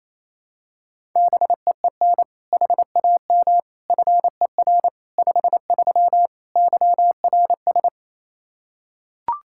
Morse code, heard as a keyed tone, about 28 wpm, 700 Hz. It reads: BEEN HAM FER 53 YRS
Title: Practice QSO - Exchange 3 - More Info - 28wpm